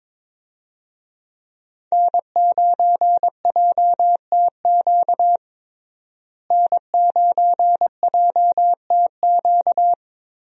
Morse code, heard as a keyed tone, about 22 wpm, 700 Hz. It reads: N9JTQ N9JTQ
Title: USA Call Signs - 5 Characters - Repeated 3x - 22wpm